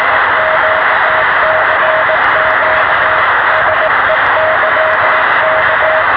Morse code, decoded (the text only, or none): none